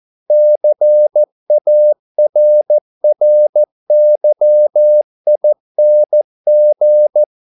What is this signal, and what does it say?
Morse code, heard as a keyed tone, about 14 wpm, 600 Hz. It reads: CARRYING